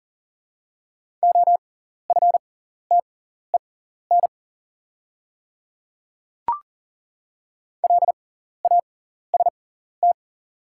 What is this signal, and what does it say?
Morse code, heard as a keyed tone, about 40 wpm, spaced as for 14 wpm, 700 Hz. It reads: OFTEN LAST
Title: 6x Character Spacing - Top 300 Words - 40wpm